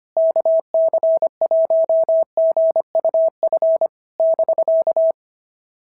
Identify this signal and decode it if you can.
Morse code, heard as a keyed tone, about 22 wpm, 650 Hz. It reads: KC1GUF <BK>
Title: SOTA QSO Practice 25wpm